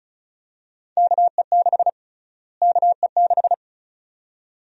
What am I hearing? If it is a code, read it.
Morse code, heard as a keyed tone, about 36 wpm, 700 Hz. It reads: KE6 KE6
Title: USA Call Sign Prefixes - Repeated 2x - 35wpm